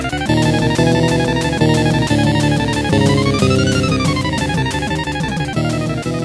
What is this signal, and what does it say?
There is no code.